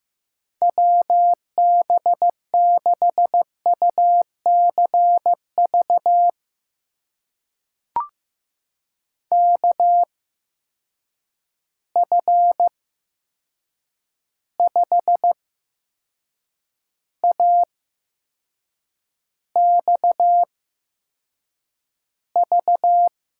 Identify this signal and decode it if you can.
Morse code, heard as a keyed tone, about 15 wpm, spaced as for 6 wpm, 700 Hz. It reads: WB6UCV KF5AXV
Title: USA Call Signs - 6 Characters - 8x Character Spacing - 15wpm